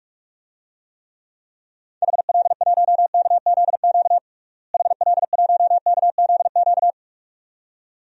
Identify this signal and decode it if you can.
Morse code, heard as a keyed tone, about 45 wpm, 700 Hz. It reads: HL1KZQ HL1KZQ
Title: DXCC-100 Call Signs - 6 Characters - Repeated 2x - 45wpm